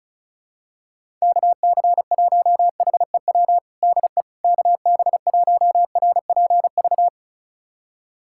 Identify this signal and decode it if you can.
Morse code, heard as a keyed tone, about 34 wpm, 690 Hz. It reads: KC1HEW DE KB1RPV
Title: USA Callsign DE Callsign - 6 Characters - 35wpm